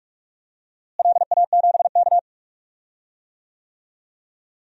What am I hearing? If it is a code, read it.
Morse code, heard as a keyed tone, about 46 wpm, 700 Hz. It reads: LA7K
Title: DXCC-100 Call Signs - 4 Characters - Repeated 2x - 45wpm